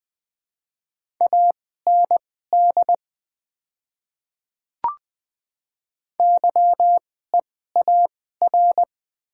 Morse code, heard as a keyed tone, about 20 wpm, spaced as for 15 wpm, 700 Hz. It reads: AND YEAR
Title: 2x Character Spacing - Top 200 Words - 20wpm